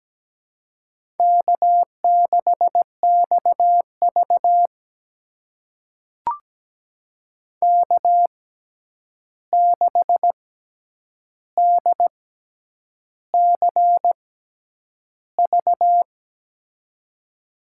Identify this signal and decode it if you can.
Morse code, heard as a keyed tone, about 17 wpm, spaced as for 8 wpm, 700 Hz. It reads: K6XV K6DCV
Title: USA Call Signs - 6x Character Spacing - 17wpm